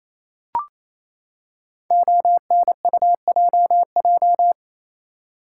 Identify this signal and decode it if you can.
Morse code, heard as a keyed tone, about 28 wpm, 700 Hz. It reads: ONUJJ
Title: Code Groups - 5 Letters - 28wpm